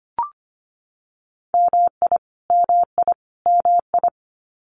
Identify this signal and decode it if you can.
Morse code, heard as a keyed tone, about 25 wpm, 700 Hz. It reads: MI MI MI
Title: US States Repeated 3x 25wpm